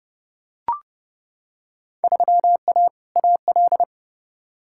Morse code, heard as a keyed tone, about 32 wpm, 700 Hz. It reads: 3A AL
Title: Field Day Exchanges - 30wpm